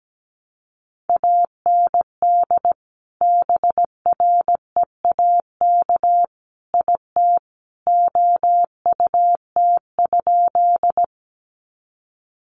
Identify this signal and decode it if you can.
Morse code, heard as a keyed tone, about 17 wpm, 700 Hz. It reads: AND BREAK IT OUT?
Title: Sentences from Top 900 Words 17wpm